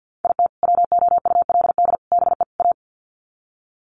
Morse code, heard as a keyed tone, about 50 wpm, 700 Hz. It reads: IT WOULD BE A
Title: Sets of 4 Words 50wpm